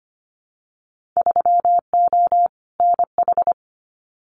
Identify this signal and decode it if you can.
Morse code, heard as a keyed tone, about 25 wpm, 700 Hz. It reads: 3O NH
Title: Winter Field Day Exchanges Encore 25wpm